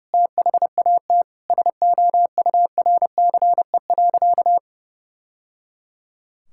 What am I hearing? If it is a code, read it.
Morse code, heard as a keyed tone, about 30 wpm, 700 Hz. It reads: THAT SOURCE.
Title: Sentences from Top 1000 Words - Review Part 5 - 30wpm